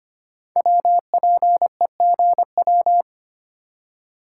Morse code, heard as a keyed tone, about 25 wpm, 700 Hz. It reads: WPEGW
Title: Code Groups - 5 Letter:Numbers - 25wpm